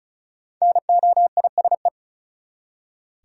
Morse code, heard as a keyed tone, about 34 wpm, 700 Hz. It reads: NOISE
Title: Five Letter Words - 35wpm